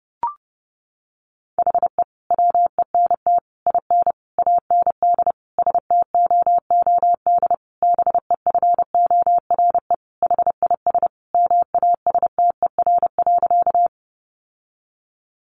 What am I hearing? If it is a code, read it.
Morse code, heard as a keyed tone, about 30 wpm, 700 Hz. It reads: HE WENT IN AND STOOD BEFORE HIS MASTER.
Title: Sentences from Top 1300 Words 30wpm